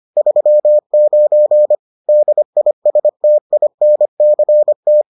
Morse code, heard as a keyed tone, about 25 wpm, 600 Hz. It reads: 39 DISTINCT